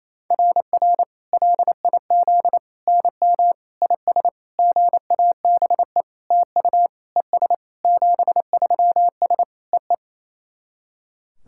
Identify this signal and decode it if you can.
Morse code, heard as a keyed tone, about 28 wpm, 700 Hz. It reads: RR LIZ NM IS GABE TU ES 73S EE
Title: SOTA QSO Practice 28wpm